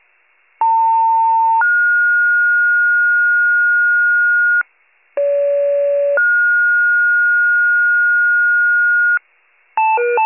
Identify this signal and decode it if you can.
Two-tone page, heard as a one-second tone, then a three-second tone, about 900 then 1430 Hz, about 570 then 1430 Hz. It is heard from 0.6 to 9.2 s.
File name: 78 FIRE